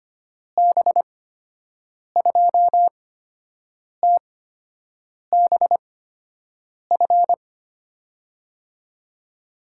Morse code, heard as a keyed tone, about 25 wpm, spaced as for 7 wpm, 700 Hz. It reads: B2TBF
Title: USA Call Signs - 8x Character Spacing - 25wpm